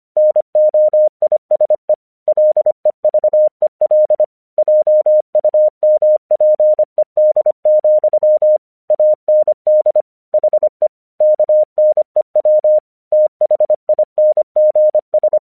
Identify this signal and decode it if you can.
Morse code, heard as a keyed tone, about 26 wpm, 610 Hz. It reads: NOISE LEVEL JUMPED, AND HE KNEW THINGS
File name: 1 English 25wpm / AQC28